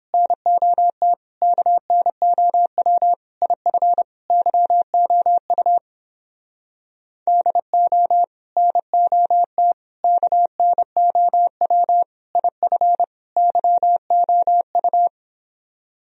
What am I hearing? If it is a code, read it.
Morse code, heard as a keyed tone, about 29 wpm, 700 Hz. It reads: NOT KNOW IF YOU DO NOT KNOW IF YOU